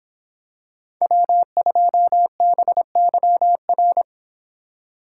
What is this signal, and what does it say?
Morse code, heard as a keyed tone, about 25 wpm, 700 Hz. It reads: W2BYR